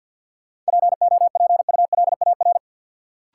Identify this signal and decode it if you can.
Morse code, heard as a keyed tone, about 53 wpm, 700 Hz. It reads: POPULAR
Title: Sentences from Top 1000 Words - Review Part 3 - New Word Drill - 50wpm